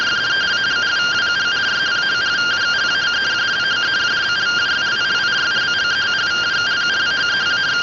Two-tone teleprinter signal, about 170 Hz shift.